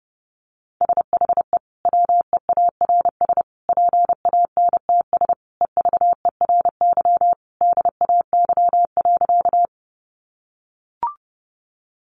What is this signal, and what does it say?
Morse code, heard as a keyed tone, about 30 wpm, 700 Hz. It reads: SHE WEARS PANTS EVERY DAY.
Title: Additional Easy Phrases 30wpm